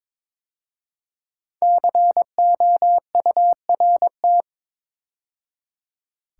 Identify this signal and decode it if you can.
Morse code, heard as a keyed tone, about 22 wpm, 700 Hz. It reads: COURT
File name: 5 Letter Words - First 3 Letters Given - 22wpm